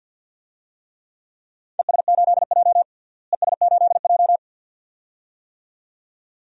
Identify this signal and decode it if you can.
Morse code, heard as a keyed tone, about 47 wpm, 700 Hz. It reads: ES8J ES8J
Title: DXCC-100 Call Signs - 4 Characters - Repeated 2x - 50wpm